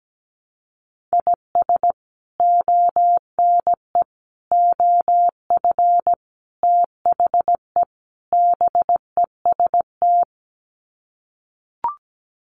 Morse code, heard as a keyed tone, about 17 wpm, 700 Hz. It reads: IS ONE OF THE BEST
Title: Sets of 5 Words 17wpm